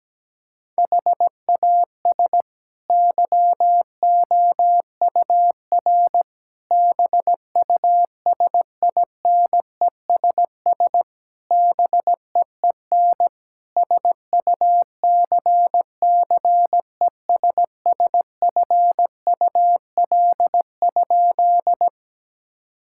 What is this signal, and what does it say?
Morse code, heard as a keyed tone, about 17 wpm, 700 Hz. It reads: HAS YOUR BUSINESS BEEN SUCCESSFUL?
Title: Sentences from Top 1000 Words - Review Part 6 - 17wpm